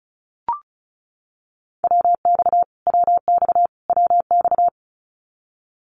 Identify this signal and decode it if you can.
Morse code, heard as a keyed tone, about 34 wpm, 700 Hz. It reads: WX WX WX
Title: Abbreviations Repeated 3x - 35wpm